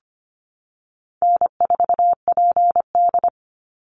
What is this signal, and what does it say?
Morse code, heard as a keyed tone, about 25 wpm, 700 Hz. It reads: N4PD
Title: Easy Call Signs Repeated 3x - 25wpm